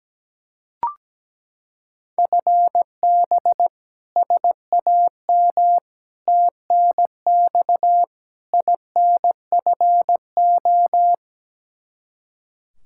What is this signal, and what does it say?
Morse code, heard as a keyed tone, about 17 wpm, 700 Hz. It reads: FB SAM TNX INFO
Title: Practice QSO - Exchange 3 - More Info - 17wpm